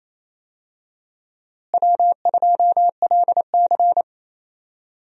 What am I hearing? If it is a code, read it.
Morse code, heard as a keyed tone, about 28 wpm, 700 Hz. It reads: W2LC
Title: USA Call Signs - 4 Characters - Repeated 2x - 28wpm